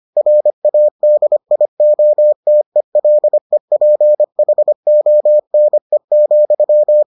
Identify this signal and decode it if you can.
Morse code, heard as a keyed tone, about 25 wpm, 600 Hz. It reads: RADIOTELEPHONE,